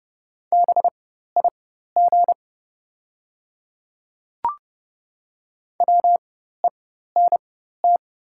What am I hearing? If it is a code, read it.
Morse code, heard as a keyed tone, about 30 wpm, spaced as for 14 wpm, 700 Hz. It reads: BIG WENT
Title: 4x Character Spacing - Top 200 Words - 30wpm